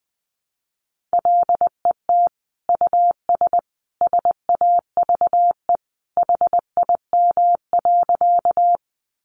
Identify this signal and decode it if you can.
Morse code, heard as a keyed tone, about 20 wpm, 700 Hz. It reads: LET US SAVE HIM.
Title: Sentences from Top 1100 Words 20wpm